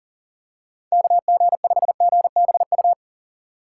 Morse code, heard as a keyed tone, about 37 wpm, 700 Hz. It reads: KG5GBU
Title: USA Call Signs - 3x Character Spacing - 40wpm